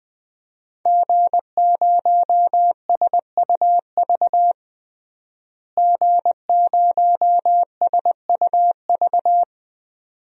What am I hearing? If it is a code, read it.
Morse code, heard as a keyed tone, about 20 wpm, 700 Hz. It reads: G0SUV G0SUV
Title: DXCC-100 Call Signs - 5 Characters - Repeated 2x - 20wpm